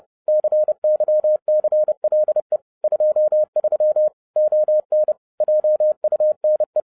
Morse code, heard as a keyed tone, about 30 wpm, 610 Hz. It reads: CYCLE 23 ON JUNE
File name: 30wpm